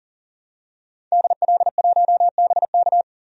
Morse code, heard as a keyed tone, about 40 wpm, 700 Hz. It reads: DL1BK
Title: DXCC-100 Call Signs - 5 Characters - 40wpm